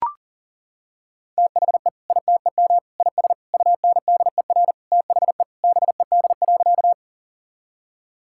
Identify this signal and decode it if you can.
Morse code, heard as a keyed tone, about 40 wpm, 700 Hz. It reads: THE ITEM IS UNDER THE BED.
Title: Sentences from Top 1000 Words - Review Part 6 - 40wpm